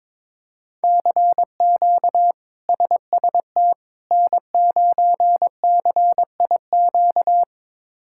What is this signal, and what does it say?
Morse code, heard as a keyed tone, about 22 wpm, 700 Hz. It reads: CQ SST N9CIQ
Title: K1USN SST QSO Practice - 22wpm